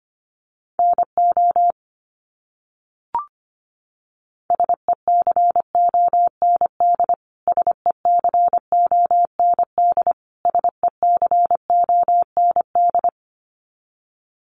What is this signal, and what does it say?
Morse code, heard as a keyed tone, about 25 wpm, 700 Hz. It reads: NO SECOND SECOND SECOND
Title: Top 500 Words Repeated 25wpm